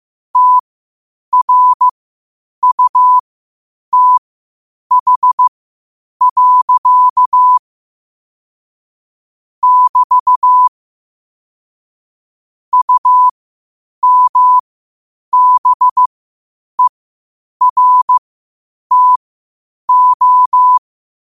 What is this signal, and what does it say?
Morse code, heard as a keyed tone, about 15 wpm, spaced as for 8 wpm, 1000 Hz. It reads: TRUTH. <BT> UMBERTO
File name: qotdfiles / 8wpm